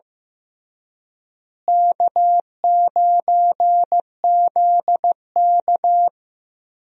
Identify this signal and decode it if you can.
Morse code, heard as a keyed tone, about 16 wpm, 700 Hz. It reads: K9ZK
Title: USA Call Signs - 4 Characters - Repeated 2x - 15wpm